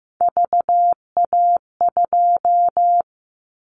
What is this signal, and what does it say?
Morse code, heard as a keyed tone, about 15 wpm, 700 Hz. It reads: VA2
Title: Callsign Prefixes for Top 25 DXCC - 15wpm